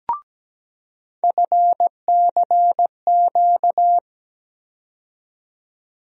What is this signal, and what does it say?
Morse code, heard as a keyed tone, about 17 wpm, 700 Hz. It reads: FCQ
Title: USA Call Sign Postfixes - 3 Letters 17wpm